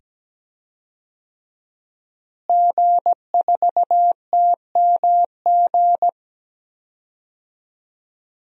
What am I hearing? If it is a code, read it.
Morse code, heard as a keyed tone, about 17 wpm, 700 Hz. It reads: G4TMG
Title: DXCC-100 Call Signs - 5 Characters - Repeated 2x - 17wpm